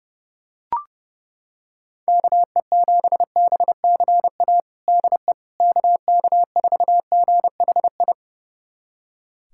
Morse code, heard as a keyed tone, about 30 wpm, 700 Hz. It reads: KE7BCA DE KK4GHI
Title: USA Callsign DE Callsign 30wpm